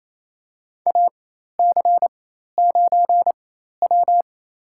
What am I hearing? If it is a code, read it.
Morse code, heard as a keyed tone, about 28 wpm, spaced as for 13 wpm, 700 Hz. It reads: AC9W
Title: USA Call Signs - 4 Characters - 4x Character Spacing - 28wpm